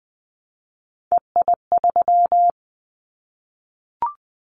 Morse code, heard as a keyed tone, about 20 wpm, 700 Hz. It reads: EI3
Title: Callsign Prefixes for Top 50 DXCC - 20wpm